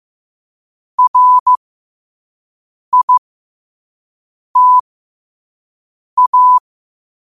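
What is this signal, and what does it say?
Morse code, heard as a keyed tone, about 15 wpm, spaced as for 5 wpm, 1000 Hz. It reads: RITA
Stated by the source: Morse quote of the day at 5 wpm.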